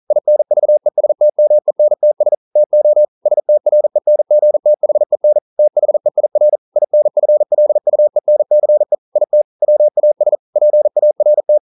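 Morse code, heard as a keyed tone, about 42 wpm, 600 Hz. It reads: INVESTMENTS TO STRENGTHEN THEIR INFLUENCE IT WAS PART
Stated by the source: Morse code practice at 41 wpm, from CW med Gnister.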